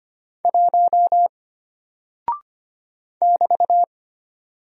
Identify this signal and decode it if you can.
Morse code, heard as a keyed tone, about 25 wpm, 700 Hz. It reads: 1 <BT>
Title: Single Character (Letter:Number:Punctuation:Prosign) Alt 25wpm